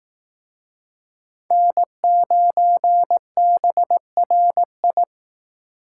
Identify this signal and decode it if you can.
Morse code, heard as a keyed tone, about 18 wpm, 700 Hz. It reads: N9BRI
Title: Speed Racing - USA Call Signs 15wpm